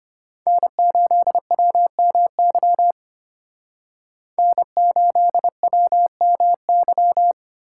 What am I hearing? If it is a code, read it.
Morse code, heard as a keyed tone, about 32 wpm, 700 Hz. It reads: N8WMY N8WMY